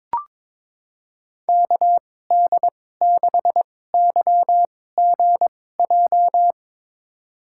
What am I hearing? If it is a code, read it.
Morse code, heard as a keyed tone, about 22 wpm, spaced as for 16 wpm, 700 Hz. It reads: KD6YGJ
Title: USA Call Signs - 6 Characters - 2x Character Spacing - 22wpm